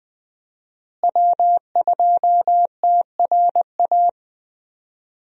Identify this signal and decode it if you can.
Morse code, heard as a keyed tone, about 20 wpm, 700 Hz. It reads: W2TRA
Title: USA Call Signs - 5 Characters - Repeated 3x - 20wpm